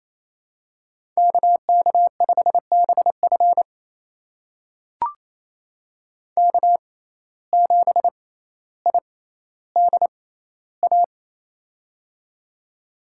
Morse code, heard as a keyed tone, about 27 wpm, spaced as for 14 wpm, 700 Hz. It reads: KK5BF K7IDA
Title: USA Call Signs - 5 Characters - 6x Character Spacing - 28wpm